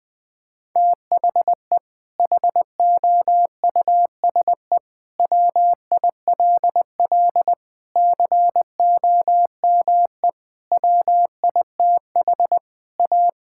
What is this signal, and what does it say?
Morse code, heard as a keyed tone, about 20 wpm, 700 Hz. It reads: THE HOUSE WILL COME WITH A
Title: Sentences from Top 1600 Words 20wpm